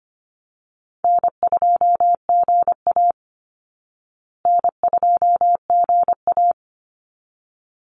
Morse code, heard as a keyed tone, about 25 wpm, 700 Hz. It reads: N2GA N2GA
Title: Easy Call Signs Repeated 2x - 25wpm